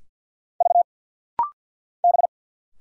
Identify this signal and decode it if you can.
Morse code, heard as a keyed tone, about 50 wpm, 700 Hz. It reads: V B
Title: Single Letters - Rapid-Fire 50wpm